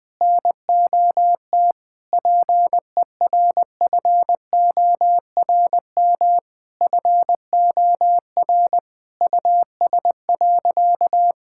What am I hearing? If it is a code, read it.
Morse code, heard as a keyed tone, about 20 wpm, 700 Hz. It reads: NOT PERFORM FOR US.
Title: Sentences from Top 1000 Words - Review Part 5 - 20wpm